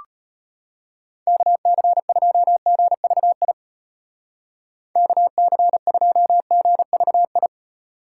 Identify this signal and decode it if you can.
Morse code, heard as a keyed tone, about 32 wpm, 700 Hz. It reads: KC2GVI KC2GVI